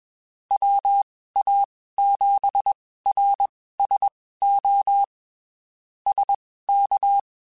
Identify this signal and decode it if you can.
Morse code, heard as a keyed tone, about 21 wpm, spaced as for 15 wpm, 800 Hz. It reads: WA7RSO SK
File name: (spaced) / QSOs Spaced-mp3s / kc7sac @20 sp